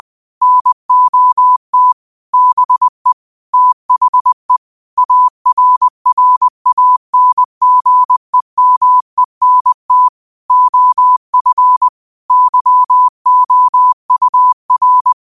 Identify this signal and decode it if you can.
Morse code, heard as a keyed tone, about 20 wpm, 1000 Hz. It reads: NOT BE THE ARRANGEMENT OF YOUR
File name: qotdfiles / 20wpm